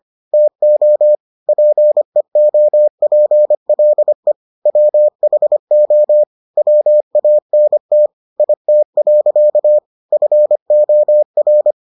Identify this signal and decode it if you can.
Morse code, heard as a keyed tone, about 25 wpm, 600 Hz. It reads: TO PEOPLE WHO WANT IT. FOR